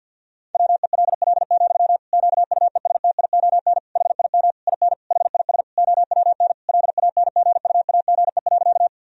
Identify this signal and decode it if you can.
Morse code, heard as a keyed tone, about 52 wpm, 700 Hz. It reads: WELL, QUESTION HIM IN HIS OWN LANGUAGE.
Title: Sentences from Top 1000 Words - Review Part 3 - 50wpm